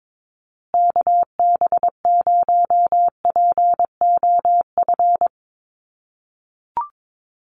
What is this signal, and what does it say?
Morse code, heard as a keyed tone, about 22 wpm, 700 Hz. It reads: KB0POF
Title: USA Call Signs Repeated 2x 22wpm